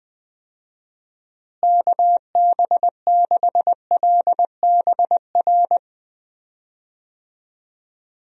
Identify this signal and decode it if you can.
Morse code, heard as a keyed tone, about 20 wpm, 700 Hz. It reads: KB6LBR
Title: USA Call Signs - 6 Characters - Repeated 2x - 20wpm